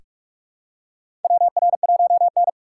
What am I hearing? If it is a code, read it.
Morse code, heard as a keyed tone, about 46 wpm, 700 Hz. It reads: WR1N
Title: USA Call Signs - 4 Characters - 4x Character Spacing - 45wpm